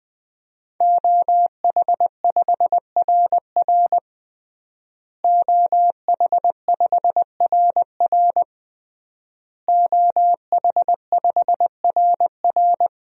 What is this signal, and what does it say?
Morse code, heard as a keyed tone, about 20 wpm, 700 Hz. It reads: OH5RR OH5RR OH5RR
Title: DXCC-100 Call Signs - 5 Characters - Repeated 3x - 20wpm